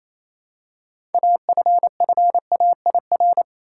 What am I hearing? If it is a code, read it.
Morse code, heard as a keyed tone, about 28 wpm, 700 Hz. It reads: AFFAIR
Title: Six Letter Words - 28wpm